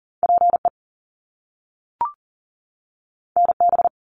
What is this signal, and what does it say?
Morse code, heard as a keyed tone, about 41 wpm, 700 Hz. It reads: PE NB
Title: Canadian Provinces and Territories 40wpm